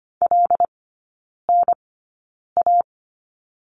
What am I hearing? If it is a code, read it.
Morse code, heard as a keyed tone, about 25 wpm, spaced as for 9 wpm, 700 Hz. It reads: LNA
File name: Lesson 15e - Number 5 - Three Characters 25wpm